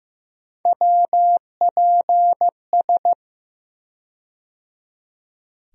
Morse code, heard as a keyed tone, about 15 wpm, 700 Hz. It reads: WPS
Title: USA Call Sign Postfixes - 3 Letters 15wpm